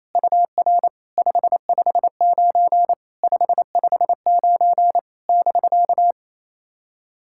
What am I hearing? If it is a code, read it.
Morse code, heard as a keyed tone, about 28 wpm, 690 Hz. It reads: UR 559 559 <BK>
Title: SOTA QSO Practice 28wpm